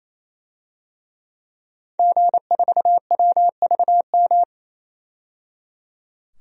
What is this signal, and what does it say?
Morse code, heard as a keyed tone, about 28 wpm, 700 Hz. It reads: G4WVM
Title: DXCC-100 Call Signs - 5 Characters - 28wpm